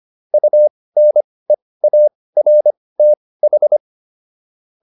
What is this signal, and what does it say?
Morse code, heard as a keyed tone, about 25 wpm, spaced as for 18 wpm, 600 Hz. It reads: UNEARTH